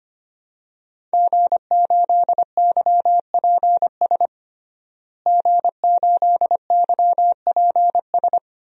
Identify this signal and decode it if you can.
Morse code, heard as a keyed tone, about 25 wpm, 700 Hz. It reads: G8YPS G8YPS
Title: DXCC-100 Call Signs - 5 Characters - Repeated 2x - 25wpm